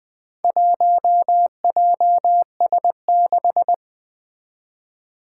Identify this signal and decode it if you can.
Morse code, heard as a keyed tone, about 20 wpm, 700 Hz. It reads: 1JS6
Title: Four Challenging Letter-Number Combinations 20wpm